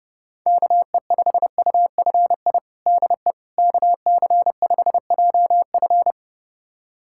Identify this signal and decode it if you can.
Morse code, heard as a keyed tone, about 30 wpm, 700 Hz. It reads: KE5UFI DE KC5JF
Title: USA Callsign DE Callsign 30wpm